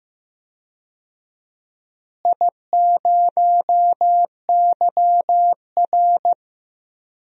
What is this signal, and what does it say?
Morse code, heard as a keyed tone, about 15 wpm, 700 Hz. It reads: I0YR
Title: DXCC-100 Call Signs - 4 Characters - Repeated 3x - 15wpm